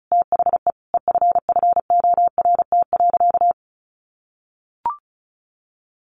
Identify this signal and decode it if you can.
Morse code, heard as a keyed tone, about 34 wpm, 700 Hz. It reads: THE EFFORT.
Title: Sentences from Top 1200 Words 35wpm